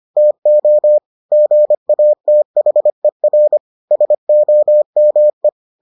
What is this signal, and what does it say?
Morse code, heard as a keyed tone, about 25 wpm, 600 Hz. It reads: TO GATHER SOME